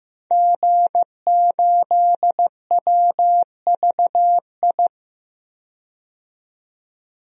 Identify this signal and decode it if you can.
Morse code, heard as a keyed tone, about 15 wpm, 700 Hz. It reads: G8WVI
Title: DXCC-100 Call Signs - 5 Characters - 15wpm